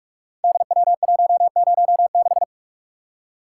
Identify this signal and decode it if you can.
Morse code, heard as a keyed tone, about 44 wpm, 690 Hz. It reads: DW106
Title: Code Groups - 5 Letter:Numbers - 45wpm